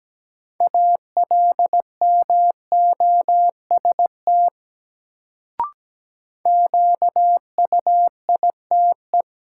Morse code, heard as a keyed tone, about 17 wpm, 700 Hz. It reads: ALMOST QUITE
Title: Sentences from Top 700 Words - New Word Drill - 17wpm